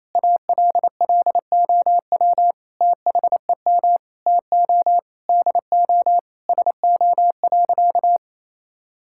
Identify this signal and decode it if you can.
Morse code, heard as a keyed tone, about 28 wpm, 700 Hz. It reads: ALLOW THEM TO DO SO.
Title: Sentences from Top 1000 Words - Review Part 5 - 28wpm